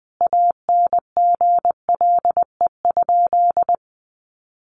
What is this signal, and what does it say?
Morse code, heard as a keyed tone, about 20 wpm, 700 Hz. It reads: ANGLE?
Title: Sentences from Top 1500 Words 20wpm